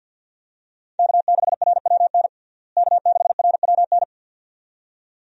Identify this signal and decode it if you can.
Morse code, heard as a keyed tone, about 52 wpm, 700 Hz. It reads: K6RWN K6RWN
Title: USA Call Signs - 5 Characters - Repeated 2x - 50wpm